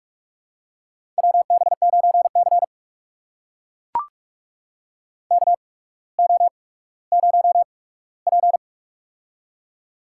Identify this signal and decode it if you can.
Morse code, heard as a keyed tone, about 44 wpm, spaced as for 19 wpm, 700 Hz. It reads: WB9C KO0P
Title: USA Call Signs - 4 Characters - 8x Character Spacing - 45wpm